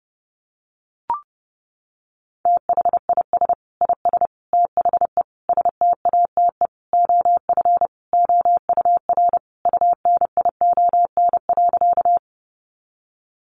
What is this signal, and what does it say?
Morse code, heard as a keyed tone, about 28 wpm, 700 Hz. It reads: THIS IS THE STATE OF OUR UNION.
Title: Sentences from Top 1400 Words 30wpm